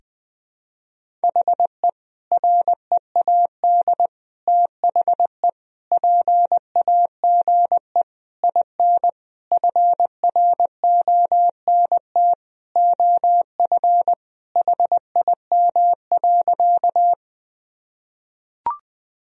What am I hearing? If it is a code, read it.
Morse code, heard as a keyed tone, about 20 wpm, 700 Hz. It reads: HE READ THE PAGE IN FRONT OF HIM.
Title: Sentences from Top 1000 Words - Review Part 1 - 20wpm